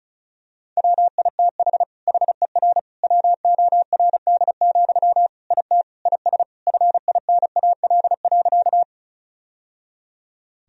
Morse code, heard as a keyed tone, about 36 wpm, 700 Hz. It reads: WITH HER WORD, IT IS FINAL.
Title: Sentences from Top 1000 Words - Review Part 2 - 35wpm